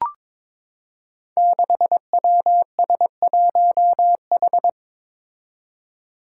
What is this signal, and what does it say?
Morse code, heard as a keyed tone, about 22 wpm, 700 Hz. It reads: 6WS1H
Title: Five Challenging Letter-Number Combinations 22wpm